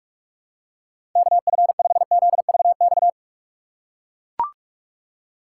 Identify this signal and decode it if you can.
Morse code, heard as a keyed tone, about 41 wpm, 700 Hz. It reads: KF5ZVX
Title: USA Call Signs - 6 Characters - 2x Character Spacing - 45wpm